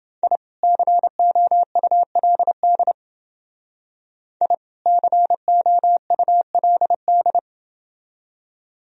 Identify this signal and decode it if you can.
Morse code, heard as a keyed tone, about 28 wpm, 700 Hz. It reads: I COULD I COULD